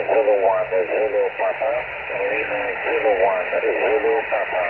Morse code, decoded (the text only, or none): none